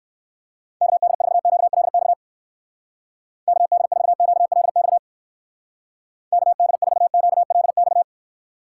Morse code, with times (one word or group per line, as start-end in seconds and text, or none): 0.8-2.1 KD4QLX
3.5-5.0 KD4QLX
6.3-8.0 KD4QLX